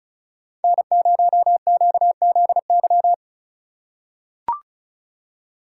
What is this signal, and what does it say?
Morse code, heard as a keyed tone, about 34 wpm, 700 Hz. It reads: N0QZY
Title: USA Call Signs - 5 Characters - 8x Character Spacing - 35wpm